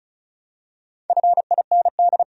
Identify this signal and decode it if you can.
Morse code, heard as a keyed tone, about 36 wpm, 700 Hz. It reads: FIND
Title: Top 100 Words 35wpm